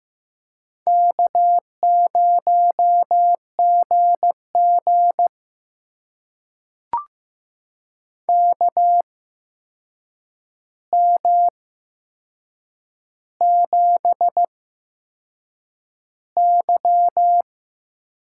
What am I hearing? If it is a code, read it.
Morse code, heard as a keyed tone, about 15 wpm, spaced as for 6 wpm, 700 Hz. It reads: K0GG KM7Y
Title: USA Call Signs - 4 Characters - 8x Character Spacing - 15wpm